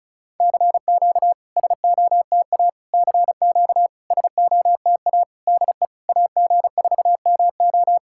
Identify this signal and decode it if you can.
Morse code, heard as a keyed tone, about 36 wpm, 700 Hz. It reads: CQ SOTA CQ SOTA DE AG4MO
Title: SOTA QSO Practice 35wpm